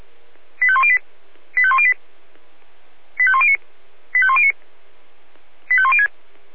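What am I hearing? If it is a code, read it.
Five-tone selective call (ZVEI), heard as quick runs of five tones, tones of about 70 ms. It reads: 85108 85108 85109 85109 85106
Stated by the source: ZVEI five-tone selective call received on an SDR.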